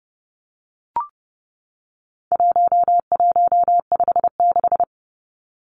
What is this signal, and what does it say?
Morse code, heard as a keyed tone, about 30 wpm, 700 Hz. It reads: 1156
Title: Numbers - 4-Digit 30wpm